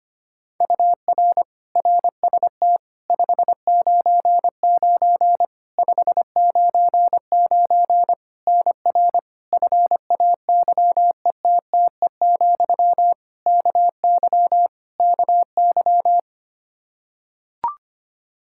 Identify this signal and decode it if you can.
Morse code, heard as a keyed tone, about 25 wpm, 700 Hz. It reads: UR RST 599 599 NR FAYETTE, KY KY
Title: Practice QSO - Exchange 1 - Beginning - 25wpm